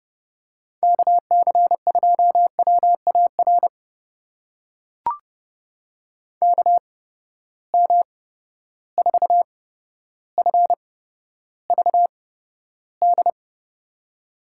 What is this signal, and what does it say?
Morse code, heard as a keyed tone, about 32 wpm, spaced as for 13 wpm, 700 Hz. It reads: KC2WAR KM4FVD
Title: USA Call Signs - 6 Characters - 8x Character Spacing - 30wpm